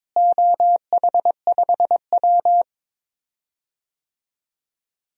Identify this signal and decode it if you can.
Morse code, heard as a keyed tone, about 22 wpm, 700 Hz. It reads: OH5W
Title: DXCC-100 Call Signs - 4 Characters - Repeated 3x - 22wpm